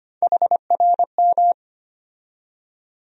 Morse code, heard as a keyed tone, about 25 wpm, 700 Hz. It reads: HRM
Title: USA Call Sign Postfixes - 3 Letters 25wpm